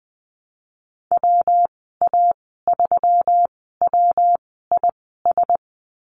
Morse code, heard as a keyed tone, about 19 wpm, spaced as for 14 wpm, 700 Hz. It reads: WA3WIS
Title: 140 Lesson 23g - Letter f - Call Signs 20wpm